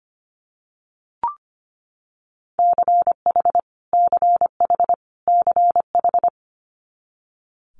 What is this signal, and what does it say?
Morse code, heard as a keyed tone, about 25 wpm, 700 Hz. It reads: CH CH CH
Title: Multi-letter Phonemes Repeated 3x 25wpm